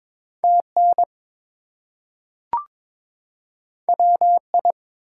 Morse code, heard as a keyed tone, about 23 wpm, 700 Hz. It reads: TN WI
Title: US States and Canadian Provinces 22wpm